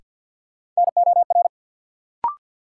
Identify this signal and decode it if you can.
Morse code, heard as a keyed tone, about 53 wpm, 700 Hz. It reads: NOR
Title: Sentences from Top 1500 Words - New Word Drill - 50wpm